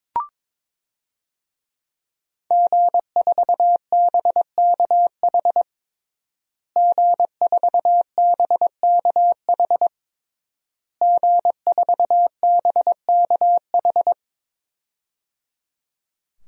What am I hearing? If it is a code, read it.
Morse code, heard as a keyed tone, about 22 wpm, 700 Hz. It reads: G4BKH G4BKH G4BKH
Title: DXCC-100 Call Signs - 5 Characters - Repeated 3x - 22wpm